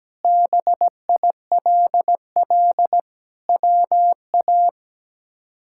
Morse code, heard as a keyed tone, about 17 wpm, 700 Hz. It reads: BILL WA
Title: K1USN SST QSO Practice - 17wpm